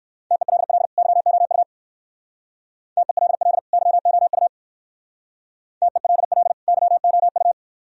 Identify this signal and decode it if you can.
Morse code, heard as a keyed tone, about 58 wpm, 700 Hz. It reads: TELL YOU TELL YOU TELL YOU